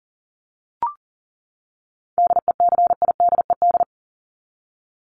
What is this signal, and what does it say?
Morse code, heard as a keyed tone, about 40 wpm, 700 Hz. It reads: DECIDED
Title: ED Suffix - Top 100 Words with Base Words 40wpm